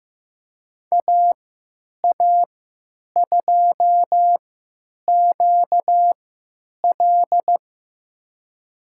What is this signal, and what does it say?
Morse code, heard as a keyed tone, about 15 wpm, spaced as for 9 wpm, 700 Hz. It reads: AA2QL
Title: USA Call Signs - 5 Characters - 3x Character Spacing - 15wpm